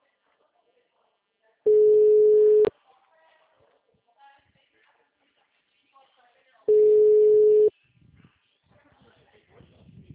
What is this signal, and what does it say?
Ringback tone (Europe, 425 Hz), with speech.